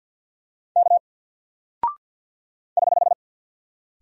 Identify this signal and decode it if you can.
Morse code, heard as a keyed tone, about 48 wpm, 700 Hz. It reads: K <HH>
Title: Single Character (Letter:Number:Punctuation:Prosign) Alt 50wpm